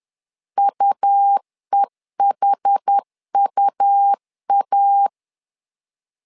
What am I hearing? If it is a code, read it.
Morse code, heard as a keyed tone, about 10 wpm, 800 Hz. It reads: UEHUA
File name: LEZIONI / 2 UHDN / 2 UH 10wpm 1714